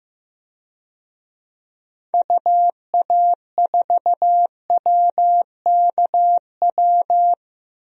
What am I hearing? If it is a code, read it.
Morse code, heard as a keyed tone, about 15 wpm, 700 Hz. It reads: UA4WKW
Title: DXCC-100 Call Signs - 6 Characters - Repeated 3x - 15wpm